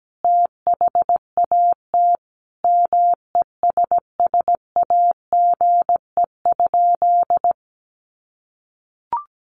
Morse code, heard as a keyed tone, about 17 wpm, 700 Hz. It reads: THAT MESSAGE?
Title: Sentences from Top 1100 Words 17wpm